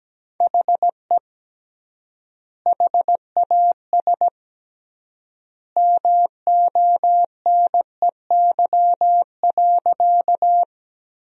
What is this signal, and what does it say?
Morse code, heard as a keyed tone, about 17 wpm, 700 Hz. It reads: HE HAS MONEY.
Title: 3x Word-Spacing - Sentences from Top 400 Words - 17wpm